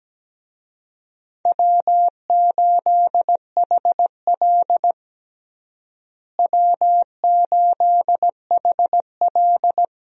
Morse code, heard as a keyed tone, about 17 wpm, 700 Hz. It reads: W8HL W8HL
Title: USA Call Signs - 4 Characters - Repeated 2x - 17wpm